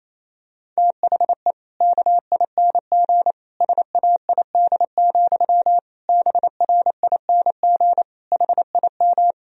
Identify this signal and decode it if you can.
Morse code, heard as a keyed tone, about 28 wpm, 700 Hz. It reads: THE KING SAID, BRING HIM
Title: Sentences from Top 1000 Words - Review Part 3 - 28wpm